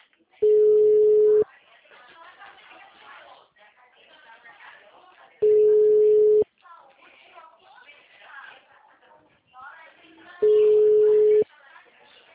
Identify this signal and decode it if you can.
Ringback tone (Europe, 425 Hz), with speech.
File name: out- -2022- - - .99316